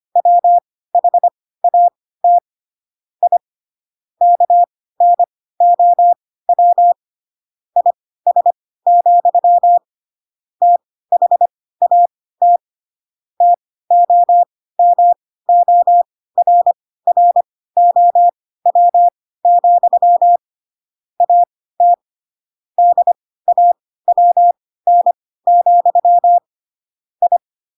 Morse code, heard as a keyed tone, about 25 wpm, spaced as for 16 wpm, 700 Hz. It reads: WHAT I KNOW IS, THAT TOMORROW, AT DAWN, I
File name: Pinocchio / Pinocchio ch4 10 25 ik1hge